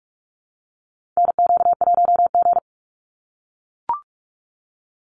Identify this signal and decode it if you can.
Morse code, heard as a keyed tone, about 44 wpm, 700 Hz. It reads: NQ1G
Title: USA Call Signs 45wpm